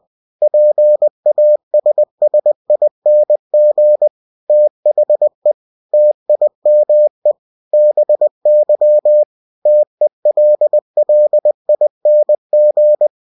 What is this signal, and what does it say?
Morse code, heard as a keyed tone, about 20 wpm, 600 Hz. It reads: PASSING THE TIME BY TELLING
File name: cw-books / winnie-the-pooh / winnie-20wpm-page-0022